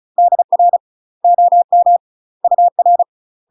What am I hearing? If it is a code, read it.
Morse code, heard as a keyed tone, about 36 wpm, 700 Hz. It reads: DR OM UR
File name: UK MASTER QSOs TRAINING / 35WPM / UK MASTER QSOs 0016